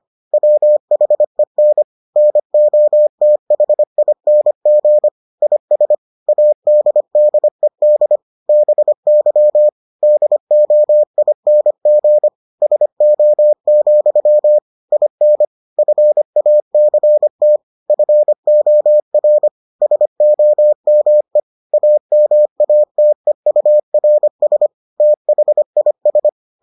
Morse code, heard as a keyed tone, about 25 wpm, 600 Hz. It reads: WHEN NOTHING IS ADDED BY DOING SO, IN FACT FOR SOME AMATEURS THIS